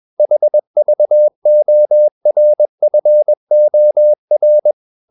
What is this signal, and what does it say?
Morse code, heard as a keyed tone, about 21 wpm, 600 Hz. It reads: HVORFOR
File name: Ferie QRS 21wpm